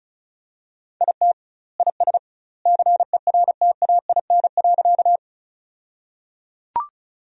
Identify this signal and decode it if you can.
Morse code, heard as a keyed tone, about 36 wpm, 700 Hz. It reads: IT IS CERTAIN.
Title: 2x Word-Spacing - Sentences from Top 500 Words - 35wpm